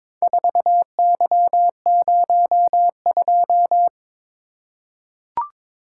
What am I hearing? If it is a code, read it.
Morse code, heard as a keyed tone, about 21 wpm, 700 Hz. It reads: 4Y02
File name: practice / 22wpm / Four Letter-Number Combinations - 22wpm